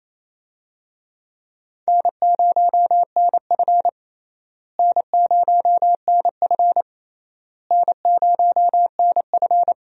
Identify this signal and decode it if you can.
Morse code, heard as a keyed tone, about 28 wpm, 700 Hz. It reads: N0NF N0NF N0NF
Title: USA Call Signs - 4 Characters - Repeated 3x - 28wpm